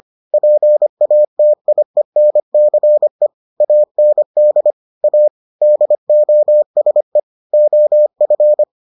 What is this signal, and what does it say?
Morse code, heard as a keyed tone, about 25 wpm, 600 Hz. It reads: PATIENCE AND A DOSE OF